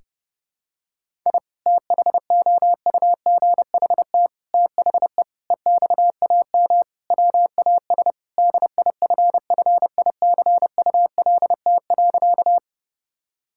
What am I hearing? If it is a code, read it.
Morse code, heard as a keyed tone, about 30 wpm, 700 Hz. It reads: I THOUGHT THE EXAM WAS DIFFICULT.
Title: Sentences from Top 1000 Words - Review Part 5 - 30wpm